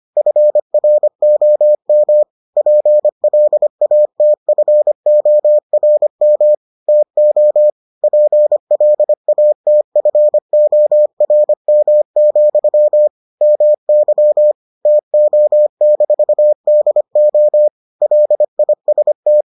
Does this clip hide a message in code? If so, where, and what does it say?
Yes: Morse code from 0.2 to 19.4 s, heard as a keyed tone, about 25 wpm, 600 Hz. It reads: FROM PLATFORM TO PLATFORM, MY TO-DO LIST